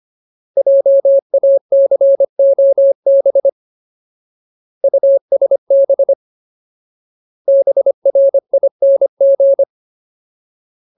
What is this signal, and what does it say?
Morse code, heard as a keyed tone, about 25 wpm, 550 Hz. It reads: JACOB USB BRING
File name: Ultimate Word List / 25WPM Ultimate